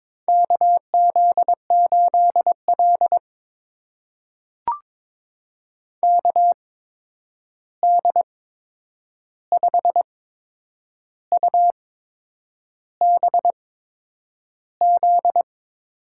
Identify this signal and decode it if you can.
Morse code, heard as a keyed tone, about 22 wpm, spaced as for 8 wpm, 700 Hz. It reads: KZ8L KD5UBZ
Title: USA Call Signs - 8x Character Spacing - 22wpm